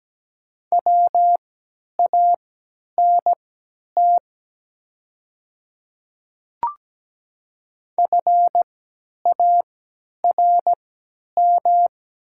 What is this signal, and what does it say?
Morse code, heard as a keyed tone, about 17 wpm, spaced as for 10 wpm, 700 Hz. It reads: WANT FARM
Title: 3x Character Spacing - Top 500 Words - 17wpm